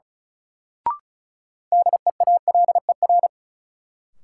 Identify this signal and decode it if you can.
Morse code, heard as a keyed tone, about 36 wpm, 700 Hz. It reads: DEALER
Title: Sentences from Top 1100 Words - New Word Drill Alt - 35wpm